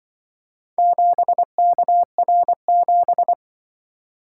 Morse code, heard as a keyed tone, about 25 wpm, 700 Hz. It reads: 7KR7
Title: Speed-Racing - Four Letter-Number Combinations - 20wpm